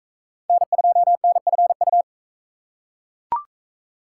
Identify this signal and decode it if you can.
Morse code, heard as a keyed tone, about 43 wpm, 700 Hz. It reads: N2NFU